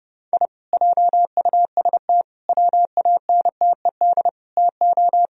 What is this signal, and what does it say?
Morse code, heard as a keyed tone, about 30 wpm, 700 Hz. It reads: I JUST WANTED TO